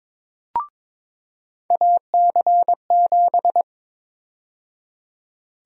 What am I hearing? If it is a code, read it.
Morse code, heard as a keyed tone, about 22 wpm, 700 Hz. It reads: AC7
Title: USA Call Sign Prefixes - 22wpm